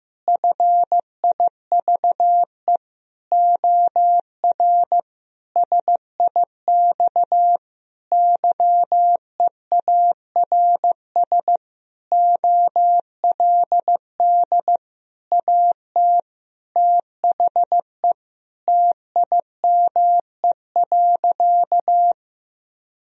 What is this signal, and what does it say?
Morse code, heard as a keyed tone, about 15 wpm, 700 Hz. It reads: FIVE OR SIX YEARS OLD AT THE TIME.
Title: Sentences from Top 1000 Words - Review Part 2 - 15wpm